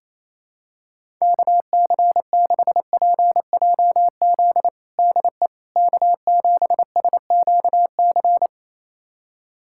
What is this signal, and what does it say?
Morse code, heard as a keyed tone, about 28 wpm, 700 Hz. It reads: KC6PJZ DE K7SQC
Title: USA Callsign DE Callsign 28wpm